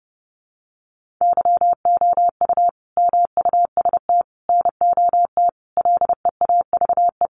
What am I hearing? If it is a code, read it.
Morse code, heard as a keyed tone, about 30 wpm, 700 Hz. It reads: YOU MUST NOT LEAVE
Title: Sentences from Top 600 Words 30wpm